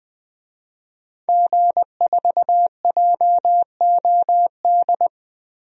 Morse code, heard as a keyed tone, about 20 wpm, 700 Hz. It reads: G4JOD
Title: DXCC-100 Call Signs - 5 Characters - Repeated 3x - 20wpm